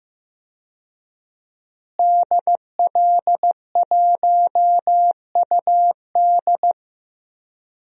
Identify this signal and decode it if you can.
Morse code, heard as a keyed tone, about 15 wpm, 700 Hz. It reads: DL1UD
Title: DXCC-100 Call Signs - 5 Characters - Repeated 3x - 15wpm